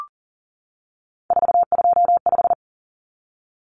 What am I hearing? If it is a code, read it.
Morse code, heard as a keyed tone, about 40 wpm, 700 Hz. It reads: 425
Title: Numbers - 3-Digit 40wpm